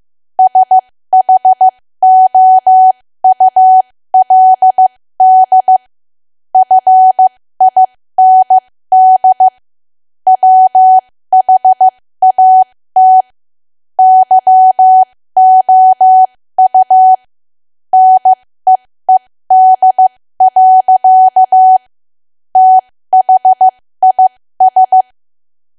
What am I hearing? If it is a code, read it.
Morse code, heard as a keyed tone, about 15 wpm, spaced as for 13 wpm, 750 Hz. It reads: SHOULD FIND WHAT YOU NEED. THIS
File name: cwa / W1AW-Level2 / 13WPM